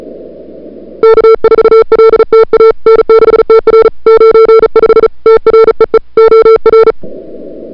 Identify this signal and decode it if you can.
Morse code, heard as a keyed tone, about 36 wpm, 1340 Hz. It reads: K4LTA N6TR 95 TREE OR
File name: CWSprint / February 2007 CW Sprint / 01-2007 CW Sprint Hour1